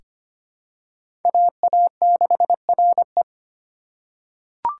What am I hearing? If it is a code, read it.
Morse code, heard as a keyed tone, about 25 wpm, 700 Hz. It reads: AA6RE
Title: K1USN SST QSO Practice - 25wpm